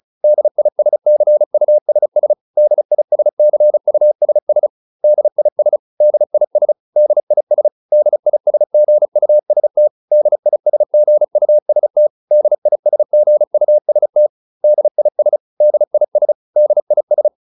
Morse code, heard as a keyed tone, about 34 wpm, 600 Hz. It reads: DISCUSS DISCUSS DIS DIS DIS DISGUST DISGUST DISGUST DIS DIS DIS